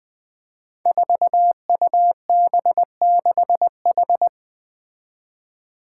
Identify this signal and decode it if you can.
Morse code, heard as a keyed tone, about 20 wpm, 700 Hz. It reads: 4UB6H
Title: Five Challenging Letter-Number Combinations 20wpm